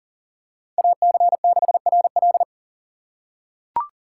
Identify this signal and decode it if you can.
Morse code, heard as a keyed tone, about 40 wpm, 700 Hz. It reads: AC6RL
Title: USA Call Signs - 5 Characters - 6x Character Spacing - 40wpm